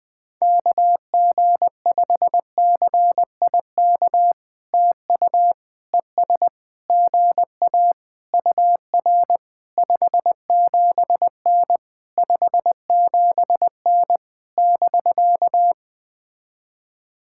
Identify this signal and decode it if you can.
Morse code, heard as a keyed tone, about 20 wpm, 700 Hz. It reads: KG5CIK TU ES GA UR 57N 57N <BK>
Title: POTA QSO Practice - 20wpm